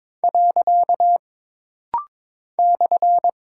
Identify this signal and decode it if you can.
Morse code, heard as a keyed tone, about 22 wpm, 700 Hz. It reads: . /
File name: practice / 22wpm / Single Punctuation-Prosign - Rapid-Fire - 22wpm